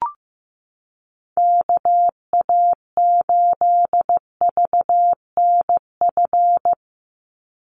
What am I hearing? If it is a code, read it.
Morse code, heard as a keyed tone, about 15 wpm, 700 Hz. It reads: KA8VNF
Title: USA Call Signs 15wpm